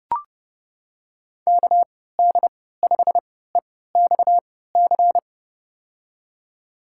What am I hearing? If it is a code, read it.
Morse code, heard as a keyed tone, about 31 wpm, spaced as for 17 wpm, 700 Hz. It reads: KD5EXC
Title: USA Call Signs - 3x Character Spacing - 30wpm